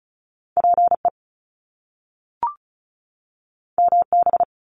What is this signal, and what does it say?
Morse code, heard as a keyed tone, about 36 wpm, 700 Hz. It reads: PE MB
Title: Canadian Provinces and Territories 35wpm